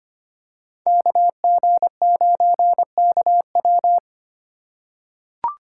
Morse code, heard as a keyed tone, about 25 wpm, 700 Hz. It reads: KG9KW
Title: USA Call Signs - 5 Characters - 2x Character Spacing - 25wpm